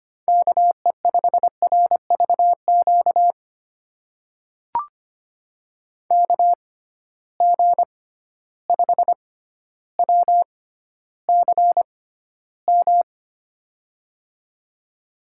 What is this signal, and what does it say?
Morse code, heard as a keyed tone, about 25 wpm, spaced as for 13 wpm, 700 Hz. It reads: KE5RVQ KG5WCM
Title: USA Call Signs - 6 Characters - 6x Character Spacing - 25wpm